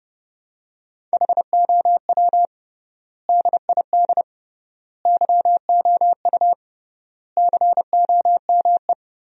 Morse code, heard as a keyed tone, about 32 wpm, 700 Hz. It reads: HOW DID YOU COME
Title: 3x Word-Spacing - Sentences from Top 100 Words - 30wpm